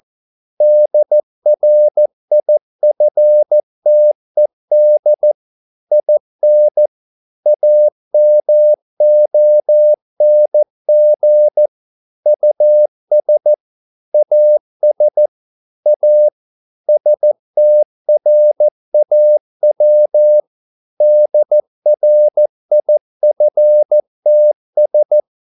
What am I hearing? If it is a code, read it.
Morse code, heard as a keyed tone, about 14 wpm, 600 Hz. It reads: DRIFTED IN AMONG US AS A STRAW DRIFTS
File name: mp3 files / a dogs mistake - 14wpm